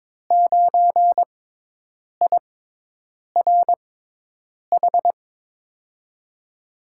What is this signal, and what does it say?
Morse code, heard as a keyed tone, about 22 wpm, spaced as for 8 wpm, 700 Hz. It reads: 9IRH
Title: USA Call Signs - 6 Characters - 6x Character Spacing - 22wpm